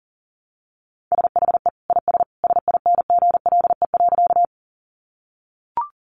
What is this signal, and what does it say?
Morse code, heard as a keyed tone, about 40 wpm, 700 Hz. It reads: SHE IS SINGLE.
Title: Sentences from Top 700 Words 40wpm